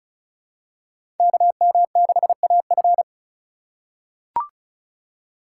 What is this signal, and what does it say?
Morse code, heard as a keyed tone, about 36 wpm, 700 Hz. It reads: KM6AF
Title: USA Call Signs - 5 Characters - 2x Character Spacing - 35wpm